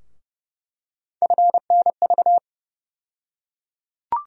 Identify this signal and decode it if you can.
Morse code, heard as a keyed tone, about 30 wpm, 700 Hz. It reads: FNV
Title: USA Call Sign Postfixes - 3 Letters 30wpm